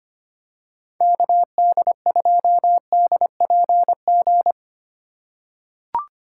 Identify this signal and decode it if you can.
Morse code, heard as a keyed tone, about 25 wpm, 700 Hz. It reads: KD2DPG
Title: USA Call Signs - 6 Characters - 2x Character Spacing - 25wpm